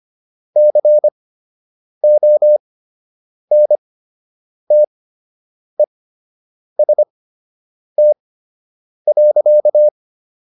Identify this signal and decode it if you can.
Morse code, heard as a keyed tone, about 25 wpm, spaced as for 8 wpm, 600 Hz. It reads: CONTEST.